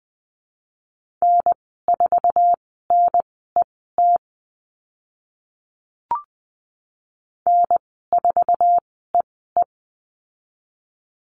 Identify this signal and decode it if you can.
Morse code, heard as a keyed tone, about 20 wpm, spaced as for 14 wpm, 700 Hz. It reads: N4NET N4EE
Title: 062 Lesson 11g - Letter h - Call Signs 20wpm